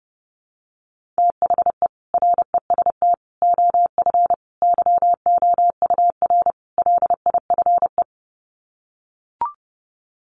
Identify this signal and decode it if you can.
Morse code, heard as a keyed tone, about 30 wpm, 700 Hz. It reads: THE REST OF YOUR LIFE
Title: Sets of 5 Words 30wpm